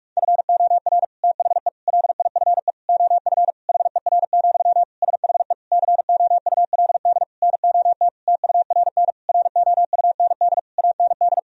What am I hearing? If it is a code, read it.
Morse code, heard as a keyed tone, about 41 wpm, 700 Hz. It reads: FOR THE LIFE OF HER, SHE COULD NOT TURN ROUND AND
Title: Sentences from Top 1000 Words - Review Part 2 - 45wpm